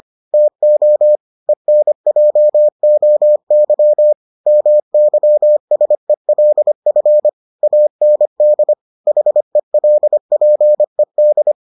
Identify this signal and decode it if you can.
Morse code, heard as a keyed tone, about 25 wpm, 600 Hz. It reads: TO ENJOY MYSELF AND HELPED